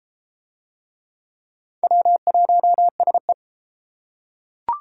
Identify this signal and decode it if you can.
Morse code, heard as a keyed tone, about 29 wpm, 700 Hz. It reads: W1SE